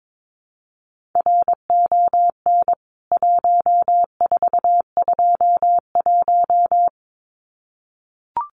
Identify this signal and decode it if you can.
Morse code, heard as a keyed tone, about 22 wpm, 700 Hz. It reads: RON 1421
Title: CWT Exchange 22wpm